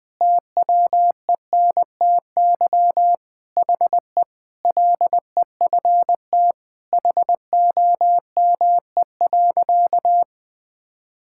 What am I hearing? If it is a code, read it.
Morse code, heard as a keyed tone, about 21 wpm, 700 Hz. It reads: TWENTY HE LEFT HOME.
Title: Sentences with 4-Word Phrases - 20wpm